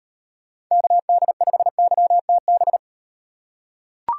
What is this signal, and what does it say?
Morse code, heard as a keyed tone, about 35 wpm, 700 Hz. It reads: KD5YTB